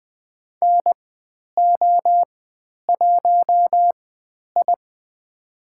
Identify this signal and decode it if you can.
Morse code, heard as a keyed tone, about 20 wpm, spaced as for 10 wpm, 700 Hz. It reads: NO1I
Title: 102 Lesson 17g - Letter u - Call Signs 20wpm